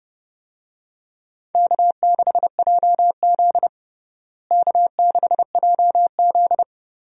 Morse code, heard as a keyed tone, about 30 wpm, 700 Hz. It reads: K6JZ K6JZ
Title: USA Call Signs - 4 Characters - Repeated 2x - 30wpm